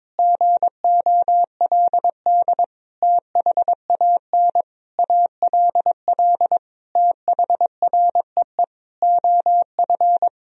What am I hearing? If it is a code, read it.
Morse code, heard as a keyed tone, about 22 wpm, 700 Hz. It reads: GOLD THAN ALL THREE OF
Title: Sentences from Top 1000 Words - Review Part 3 - 22wpm